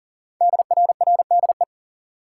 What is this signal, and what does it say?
Morse code, heard as a keyed tone, about 39 wpm, 700 Hz. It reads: DRRDE
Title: Code Groups - 5 Letters - 40wpm